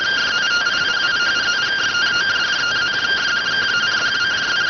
Two-tone teleprinter signal, about 170 Hz shift.